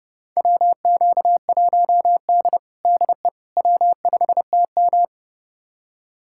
Morse code, heard as a keyed tone, about 30 wpm, 700 Hz. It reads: WQ1D DE W5TM
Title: USA Callsign DE Callsign - 4 Characters - 30wpm